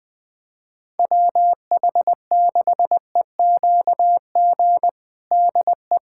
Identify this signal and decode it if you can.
Morse code, heard as a keyed tone, about 20 wpm, 700 Hz. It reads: WH6EQG DE
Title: USA Callsign DE Callsign - 6 Characters - 20wpm